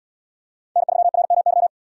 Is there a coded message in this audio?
Yes: Morse code from 0.8 to 1.7 s, heard as a keyed tone, about 77 wpm, 700 Hz. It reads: N3RMY